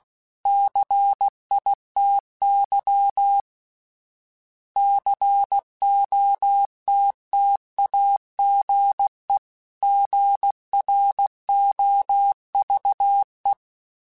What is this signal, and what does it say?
Morse code, heard as a keyed tone, about 16 wpm, 800 Hz. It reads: CITY COTTAGE GROVE
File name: fromSMs / FromSMs mp3s / Cities@15